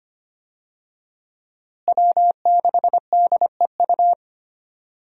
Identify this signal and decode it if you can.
Morse code, heard as a keyed tone, about 25 wpm, 700 Hz. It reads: W6DEU
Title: USA Call Signs - 5 Characters - Repeated 3x - 25wpm